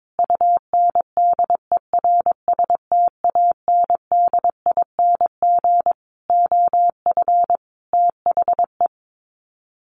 Morse code, heard as a keyed tone, about 22 wpm, 700 Hz. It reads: UNDERSTANDING OF THE
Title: Sets of 5 Words Encore2 22wpm